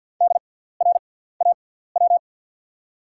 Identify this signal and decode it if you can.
Morse code, heard as a keyed tone, about 48 wpm, spaced as for 17 wpm, 690 Hz. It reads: DRAW
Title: 6x Character Spacing - Top 500 Words - 50wpm